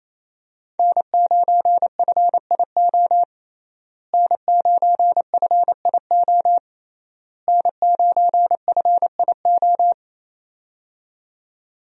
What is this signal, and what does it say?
Morse code, heard as a keyed tone, about 28 wpm, 700 Hz. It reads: N9FIO N9FIO N9FIO
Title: USA Call Signs - 5 Characters - Repeated 3x - 28wpm